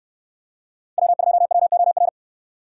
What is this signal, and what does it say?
Morse code, heard as a keyed tone, about 68 wpm, 700 Hz. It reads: W2WOR